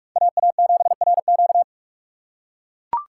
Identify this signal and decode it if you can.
Morse code, heard as a keyed tone, about 41 wpm, 700 Hz. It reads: AA7RQ